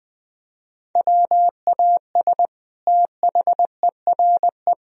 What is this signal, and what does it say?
Morse code, heard as a keyed tone, about 20 wpm, 700 Hz. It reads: WAS THERE
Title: Sentences from Top 1000 Words - Review Part 4 - 20wpm